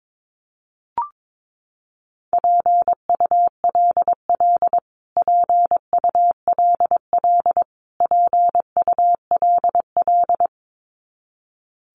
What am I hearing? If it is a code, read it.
Morse code, heard as a keyed tone, about 22 wpm, 700 Hz. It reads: PULL PULL PULL
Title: Top 500 Words Repeated 22wpm